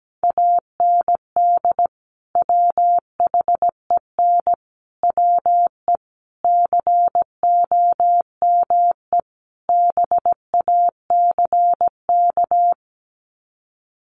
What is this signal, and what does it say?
Morse code, heard as a keyed tone, about 17 wpm, 700 Hz. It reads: AND WHEN WE COME BACK
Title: Sets of 5 Words 17wpm